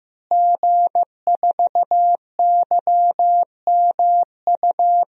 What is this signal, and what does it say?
Morse code, heard as a keyed tone, about 16 wpm, 700 Hz. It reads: G4YMU
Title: DXCC-100 Call Signs - 5 Characters - Repeated 2x - 15wpm